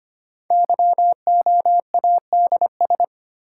Morse code, heard as a keyed tone, about 25 wpm, 700 Hz. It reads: YOADS
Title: Code Groups - 5 Letter:Numbers - 25wpm